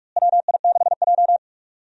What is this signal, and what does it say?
Morse code, heard as a keyed tone, about 43 wpm, 700 Hz. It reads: WI6J
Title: USA Call Signs - 4 Characters - Repeated 2x - 45wpm